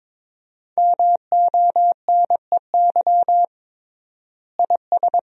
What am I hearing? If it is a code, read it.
Morse code, heard as a keyed tone, about 21 wpm, 700 Hz. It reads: MONEY IS
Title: 3x Word-Spacing - Sentences from Top 500 Words - 22wpm